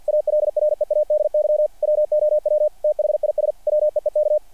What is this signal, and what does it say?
Morse code, heard as a keyed tone, about 48 wpm, 600 Hz. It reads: ALREADY WOW THIS WEEK
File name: archived / CRSnet